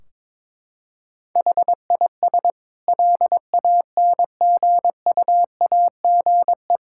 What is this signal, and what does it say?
Morse code, heard as a keyed tone, about 22 wpm, 700 Hz. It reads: HIS LANGUAGE
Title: Sentences from Top 1600 Words 22wpm